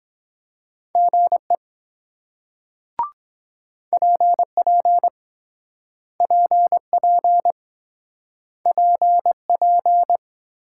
Morse code, heard as a keyed tone, about 23 wpm, 700 Hz. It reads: GE PP PP PP